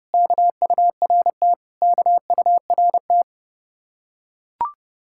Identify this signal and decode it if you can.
Morse code, heard as a keyed tone, about 30 wpm, 700 Hz. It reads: KURT KURT